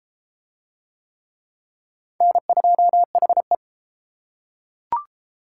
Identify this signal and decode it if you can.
Morse code, heard as a keyed tone, about 27 wpm, 700 Hz. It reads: N2HE
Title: Speed Racing - USA 1x2 Call Signs 22wpm